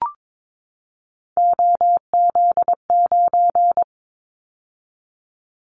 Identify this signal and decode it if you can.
Morse code, heard as a keyed tone, about 22 wpm, 700 Hz. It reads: OZ9
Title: Callsign Prefixes for Top 50 DXCC - 22wpm